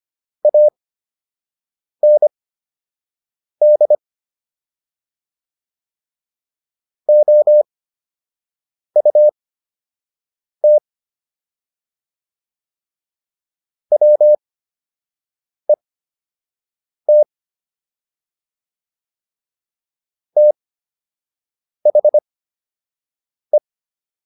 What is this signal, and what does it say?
Morse code, heard as a keyed tone, about 25 wpm, spaced as for 6 wpm, 600 Hz. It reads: AND OUT WET THE